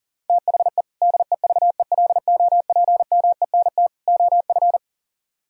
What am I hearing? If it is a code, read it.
Morse code, heard as a keyed tone, about 50 wpm, 700 Hz. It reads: THE DEVELOPMENT OF